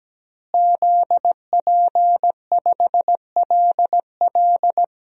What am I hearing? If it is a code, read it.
Morse code, heard as a keyed tone, about 17 wpm, 700 Hz. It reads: ZP5LL
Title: DXCC-100 Call Signs - 5 Characters - Repeated 3x - 17wpm